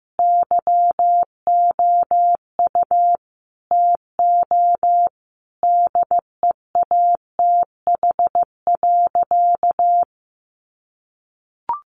Morse code, heard as a keyed tone, about 15 wpm, 700 Hz. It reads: YOU TO DEATH.
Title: Sentences from Top 1300 Words 15wpm